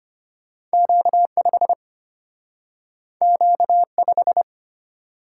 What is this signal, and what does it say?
Morse code, heard as a keyed tone, about 34 wpm, 700 Hz. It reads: Q5 Q5